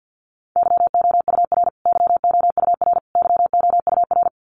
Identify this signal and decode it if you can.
Morse code, heard as a keyed tone, about 50 wpm, 700 Hz. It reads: YOUR YOUR YOUR
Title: Top 300 Words Repeated 50wpm